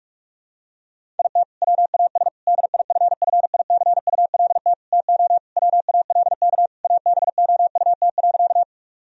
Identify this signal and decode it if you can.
Morse code, heard as a keyed tone, about 42 wpm, 700 Hz. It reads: IT WAS DIFFICULT TO WALK ABOUT.
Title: Sentences from Top 1000 Words - Review Part 1 - 45wpm